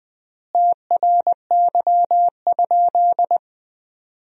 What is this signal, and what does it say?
Morse code, heard as a keyed tone, about 20 wpm, 700 Hz. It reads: TRY?
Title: 4x Word-Spacing - Sentences from Top 200 Words - 20wpm